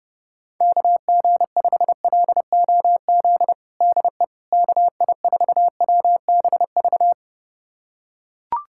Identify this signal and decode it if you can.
Morse code, heard as a keyed tone, about 31 wpm, 700 Hz. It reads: KG5LOZ DE KI4WBV
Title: USA Callsign DE Callsign - 6 Characters - 30wpm